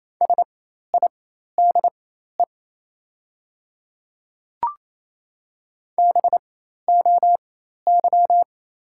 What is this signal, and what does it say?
Morse code, heard as a keyed tone, about 28 wpm, spaced as for 13 wpm, 700 Hz. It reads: SIDE BOY
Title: 4x Character Spacing - Top 200 Words - 28wpm